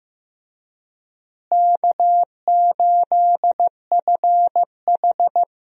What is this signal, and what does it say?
Morse code, heard as a keyed tone, about 15 wpm, 700 Hz. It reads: K8FH
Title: USA Call Signs - 4 Characters - Repeated 3x - 15wpm